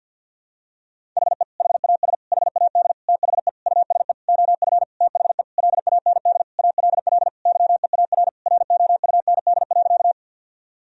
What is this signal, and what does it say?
Morse code, heard as a keyed tone, about 49 wpm, 700 Hz. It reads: HE HAS HAD THE USE OF THE LAND ALL YEAR ROUND.
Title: Sentences from Top 1000 Words - Review Part 1 - 50wpm